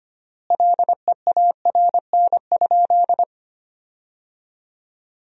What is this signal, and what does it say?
Morse code, heard as a keyed tone, about 25 wpm, 700 Hz. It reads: LEARN?
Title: 3x Word-Spacing - Sentences from Top 300 Words - 25wpm